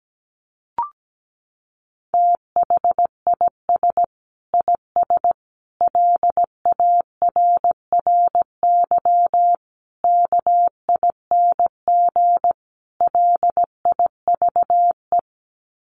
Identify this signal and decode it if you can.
Morse code, heard as a keyed tone, about 17 wpm, 700 Hz. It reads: THIS IS LARRY KING LIVE
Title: Sets of 5 Words Encore2 17wpm